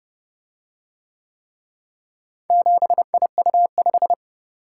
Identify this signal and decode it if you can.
Morse code, heard as a keyed tone, about 29 wpm, 700 Hz. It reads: 7IU5